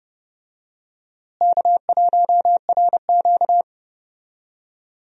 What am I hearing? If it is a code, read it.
Morse code, heard as a keyed tone, about 32 wpm, 700 Hz. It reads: K1RQ
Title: USA Call Signs - 4 Characters - Repeated 2x - 30wpm